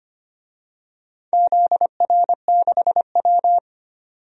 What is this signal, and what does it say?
Morse code, heard as a keyed tone, about 25 wpm, 700 Hz. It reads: ZR6W
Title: DXCC-100 Call Signs - 4 Characters - 25wpm